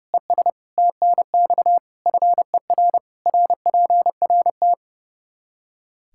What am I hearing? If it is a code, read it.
Morse code, heard as a keyed tone, about 30 wpm, 700 Hz. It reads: ES TNX FER RPRT
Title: Practice QSO - Exchange 2 - Info - 30wpm